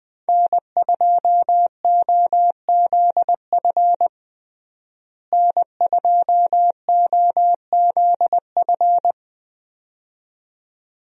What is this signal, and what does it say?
Morse code, heard as a keyed tone, about 20 wpm, 700 Hz. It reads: N2OZF N2OZF
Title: USA Call Signs - 5 Characters - Repeated 2x - 20wpm